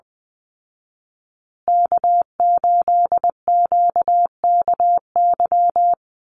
Morse code, heard as a keyed tone, about 20 wpm, 700 Hz. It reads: K8QKY
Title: Call Signs Repeated 2x - 20wpm